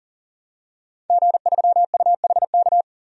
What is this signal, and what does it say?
Morse code, heard as a keyed tone, about 39 wpm, 700 Hz. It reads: G3UHK
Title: DXCC-100 Call Signs - 5 Characters - 40wpm